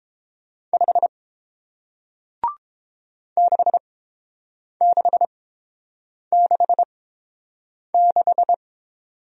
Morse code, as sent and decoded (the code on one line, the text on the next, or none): ..... / -.... -.... -.... -....
5 6666